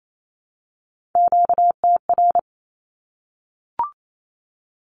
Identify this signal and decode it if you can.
Morse code, heard as a keyed tone, about 28 wpm, 700 Hz. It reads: QTR
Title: Q-Codes Repeated 3x 28wpm